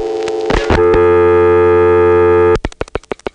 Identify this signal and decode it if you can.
Dial tone (North America), 350 + 440 Hz, with noise.
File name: phonetrips / PrinceWmCoVA5HQ